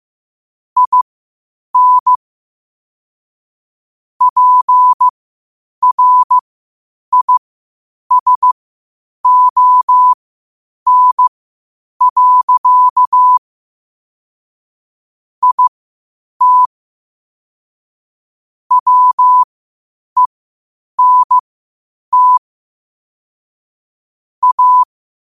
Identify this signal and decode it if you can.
Morse code, heard as a keyed tone, about 15 wpm, spaced as for 8 wpm, 1000 Hz. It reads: IN PRISON. IT WENT A
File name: qotdfiles / 8wpm